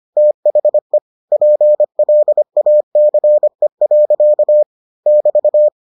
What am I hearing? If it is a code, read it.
Morse code, heard as a keyed tone, about 25 wpm, 600 Hz. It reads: THE PLACE. <BT>